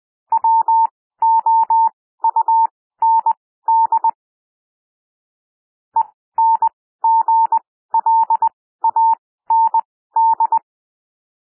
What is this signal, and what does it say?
Morse code, heard as a keyed tone, about 20 wpm, spaced as for 14 wpm, 900 Hz. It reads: WOUND ENGLAND
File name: SOUND WORDS 20 wpm